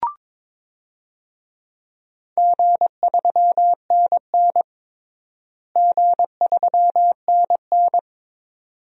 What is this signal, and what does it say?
Morse code, heard as a keyed tone, about 22 wpm, 700 Hz. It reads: G3NN G3NN
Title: DXCC-100 Call Signs - 4 Characters - Repeated 2x - 22wpm